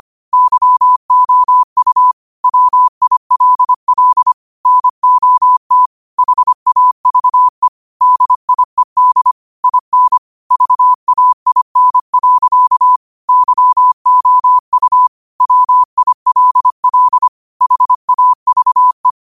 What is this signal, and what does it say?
Morse code, heard as a keyed tone, about 25 wpm, 1000 Hz. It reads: YOU WILL NOT HAVE DIED IN VAIN. YOU WILL HAVE
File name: qotdfiles / 25wpm